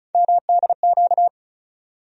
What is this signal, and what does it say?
Morse code, heard as a keyed tone, about 36 wpm, 700 Hz. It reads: MDQ
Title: USA Call Sign Postfixes - 3 Letters 35wpm